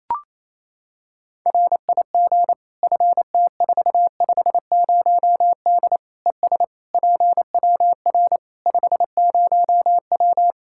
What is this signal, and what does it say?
Morse code, heard as a keyed tone, about 28 wpm, 700 Hz. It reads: RIG FT450D ES PWR 50W
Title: K1USN SST QSO Practice 28wpm